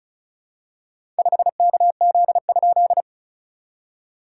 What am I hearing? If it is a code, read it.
Morse code, heard as a keyed tone, about 34 wpm, 690 Hz. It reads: 5KZ?
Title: SOTA QSO Practice 35wpm